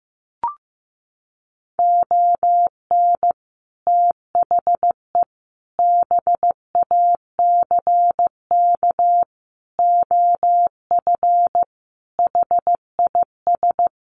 Morse code, heard as a keyed tone, about 15 wpm, 700 Hz. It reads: ON THE BACK OF HIS
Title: Sets of 5 Words Encore 15wpm